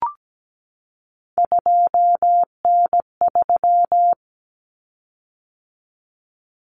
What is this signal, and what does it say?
Morse code, heard as a keyed tone, about 17 wpm, 700 Hz. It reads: 2N3
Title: Common Cut Numbers - 3-Digit 17wpm